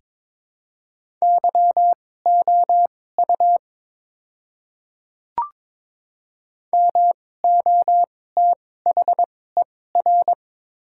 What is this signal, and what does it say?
Morse code, heard as a keyed tone, about 21 wpm, spaced as for 16 wpm, 700 Hz. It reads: YOU MOTHER
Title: 2x Character Spacing - Top 300 Words - 22wpm